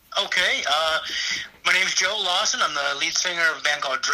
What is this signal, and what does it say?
Emergency Alert System (SAME) tones, heard as harsh buzzing bursts, with speech. It is heard from 0.3 to 3.5 s.